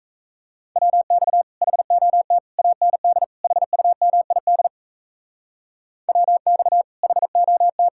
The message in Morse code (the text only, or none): WX HOT AND HUMID WX HOT